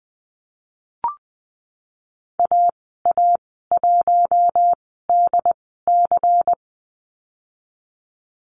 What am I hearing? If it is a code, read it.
Morse code, heard as a keyed tone, about 20 wpm, spaced as for 15 wpm, 700 Hz. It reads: AA1DC
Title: 140 Lesson 23g - Letter f - Call Signs 20wpm